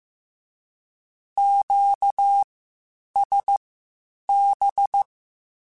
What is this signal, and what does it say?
Morse code, heard as a keyed tone, about 15 wpm, spaced as for 8 wpm, 800 Hz. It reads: QSB
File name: (spaced) / QSOs Spaced-mp3s / WD0BGP 1 @13 sp